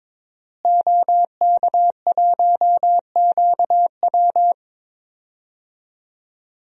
Morse code, heard as a keyed tone, about 22 wpm, 700 Hz. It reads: OK1QW
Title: DXCC-100 Call Signs - 5 Characters - Repeated 2x - 22wpm